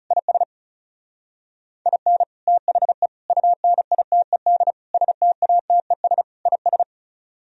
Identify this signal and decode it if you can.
Morse code, heard as a keyed tone, about 37 wpm, 700 Hz. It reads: IS IN THE UNITED STATES IS
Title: Speed-Racing - Sets of 5 Words 30wpm